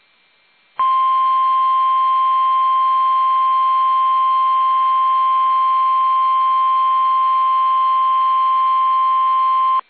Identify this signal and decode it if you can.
Emergency Alert System (SAME) tones (the 1050 Hz attention tone). It is heard from 1.2 to 9.0 s.